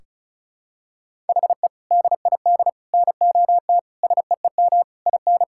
Morse code, heard as a keyed tone, about 36 wpm, 700 Hz. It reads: HE DID NOT SEEM IN
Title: Sentences from Top 1600 Words 35wpm